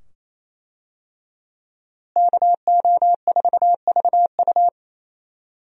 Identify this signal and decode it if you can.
Morse code, heard as a keyed tone, about 27 wpm, 700 Hz. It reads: KO4VU
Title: USA Call Signs - 5 Characters - Repeated 2x - 28wpm